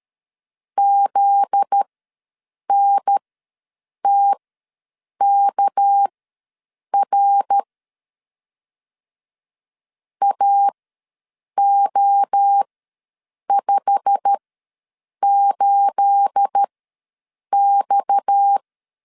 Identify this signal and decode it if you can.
Morse code, heard as a keyed tone, about 13 wpm, spaced as for 7 wpm, 800 Hz. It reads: ZNTKR AO58X
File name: LEZIONI / 9 286 ws 12wpm 3351